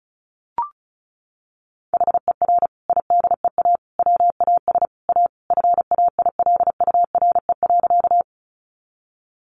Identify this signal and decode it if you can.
Morse code, heard as a keyed tone, about 34 wpm, 700 Hz. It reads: HER IDEA WAS A FAILURE.
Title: Sentences from Top 900 Words 35wpm